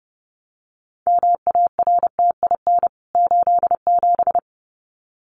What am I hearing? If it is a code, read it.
Morse code, heard as a keyed tone, about 30 wpm, 700 Hz. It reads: MARTIN 87
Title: CWT Exchange 30wpm